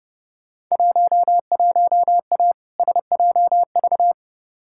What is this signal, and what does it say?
Morse code, heard as a keyed tone, about 30 wpm, 700 Hz. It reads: 11A SJV
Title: Field Day Exchanges Encore - 30wpm